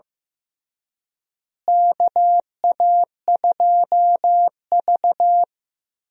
Morse code, heard as a keyed tone, about 15 wpm, 700 Hz. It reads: KA2V
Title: USA Call Signs - 4 Characters - Repeated 3x - 15wpm